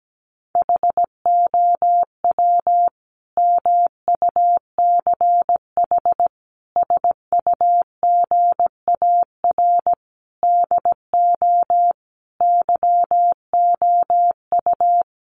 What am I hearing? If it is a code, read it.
Morse code, heard as a keyed tone, about 17 wpm, 700 Hz. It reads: HOW MUCH SUGAR DO YOU
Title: Sentences from Top 1200 Words 17wpm